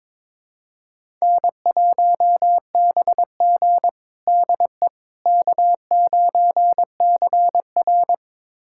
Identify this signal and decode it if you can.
Morse code, heard as a keyed tone, about 22 wpm, 700 Hz. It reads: N1BG DE K9CR
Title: USA Callsign DE Callsign - 4 Characters - 22wpm